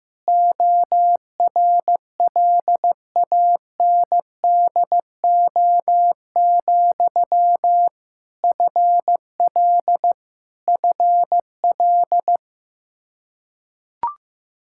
Morse code, heard as a keyed tone, about 15 wpm, 700 Hz. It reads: ORLANDO, FL FL
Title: Practice QSO - Exchange 1 - Beginning - 15wpm